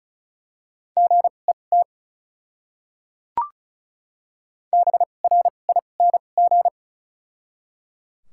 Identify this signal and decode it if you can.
Morse code, heard as a keyed tone, about 34 wpm, spaced as for 25 wpm, 700 Hz. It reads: GET BRING
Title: 2x Character Spacing - Top 500 Words - 35wpm